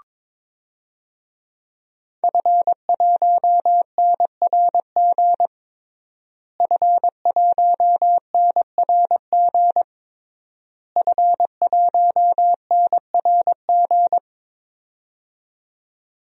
Morse code, heard as a keyed tone, about 22 wpm, 700 Hz. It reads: F1NRG F1NRG F1NRG
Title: DXCC-100 Call Signs - 5 Characters - Repeated 3x - 22wpm